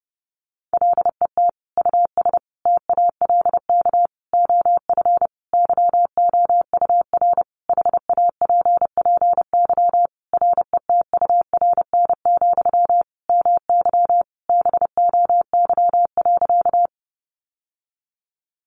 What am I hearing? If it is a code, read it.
Morse code, heard as a keyed tone, about 30 wpm, 700 Hz. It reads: LET US TALK OF YOUR HAPPY RETURN, MY BOY.
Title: Sentences from Top 1000 Words 30wpm